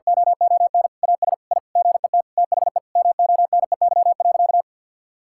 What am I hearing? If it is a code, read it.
Morse code, heard as a keyed tone, about 50 wpm, 700 Hz. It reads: OON AS I GET THE MONEY.
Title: Sentences from Top 1000 Words - Review Part 5 - 50wpm